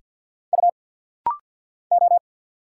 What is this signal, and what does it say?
Morse code, heard as a keyed tone, about 48 wpm, 700 Hz. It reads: U O
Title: Single Letters - Rapid-Fire 50wpm